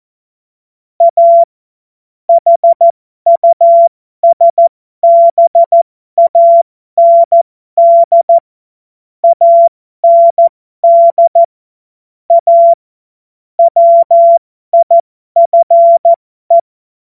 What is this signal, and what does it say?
Morse code, heard as a keyed tone, about 14 wpm, spaced as for 12 wpm, 680 Hz. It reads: A HUSBAND AND A WIFE
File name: Morsekurs / Witze